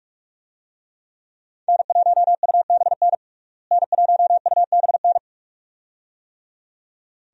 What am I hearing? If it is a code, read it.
Morse code, heard as a keyed tone, about 46 wpm, 700 Hz. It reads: N1UBN N1UBN
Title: USA Call Signs - 5 Characters - Repeated 2x - 45wpm